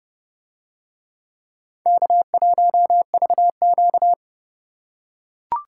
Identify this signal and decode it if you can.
Morse code, heard as a keyed tone, about 28 wpm, 700 Hz. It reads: K1VQ